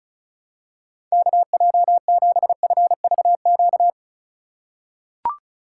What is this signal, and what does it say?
Morse code, heard as a keyed tone, about 36 wpm, 700 Hz. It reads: KJ7FVQ
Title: USA Call Signs - 3x Character Spacing - 35wpm